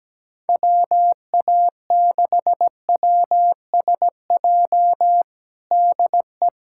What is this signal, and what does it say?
Morse code, heard as a keyed tone, about 17 wpm, 700 Hz. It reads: WA6WSJ DE
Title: USA Callsign DE Callsign - 6 Characters - 17wpm